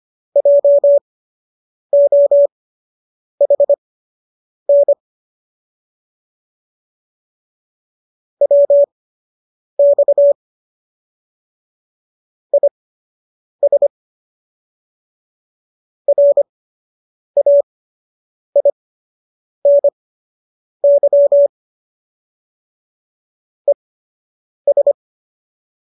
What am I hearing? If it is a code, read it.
Morse code, heard as a keyed tone, about 25 wpm, spaced as for 8 wpm, 570 Hz. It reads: JOHN WX IS RAINY ES